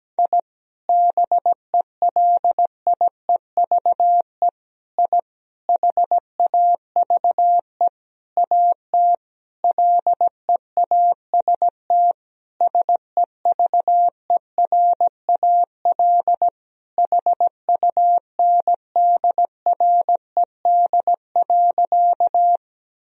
Morse code, heard as a keyed tone, about 17 wpm, 700 Hz. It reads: I BELIEVE I HAVE AT LEAST SEVERAL HUNDRED.
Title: Sentences from Top 1000 Words - Review Part 2 - 17wpm